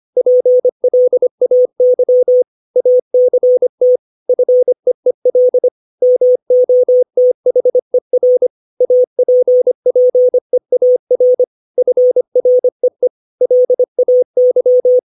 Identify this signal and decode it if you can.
Morse code, heard as a keyed tone, about 25 wpm, 500 Hz. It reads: PLAY ACT FEEL MOTHER APPEAR FREE LAY
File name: 632 words / 632 Words 25WPM